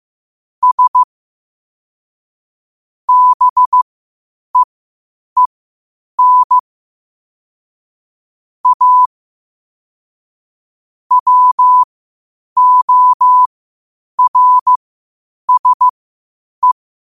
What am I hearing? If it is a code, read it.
Morse code, heard as a keyed tone, about 15 wpm, spaced as for 8 wpm, 1000 Hz. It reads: S BEEN A WORSE
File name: qotdfiles / 8wpm